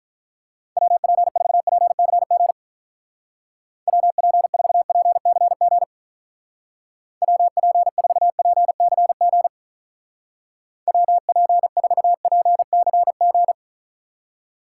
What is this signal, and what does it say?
Morse code, heard as a keyed tone, about 43 wpm, 700 Hz. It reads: WP4PCG WP4PCG WP4PCG WP4PCG
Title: Speed Racing - USA 2x3 Call Signs 35wpm